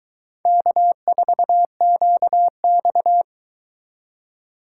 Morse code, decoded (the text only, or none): K4QX